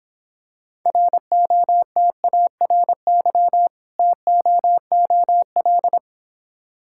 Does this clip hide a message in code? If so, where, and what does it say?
Yes: Morse code from 0.9 to 6.0 s, heard as a keyed tone, about 25 wpm, 700 Hz. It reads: ROTARY TOOL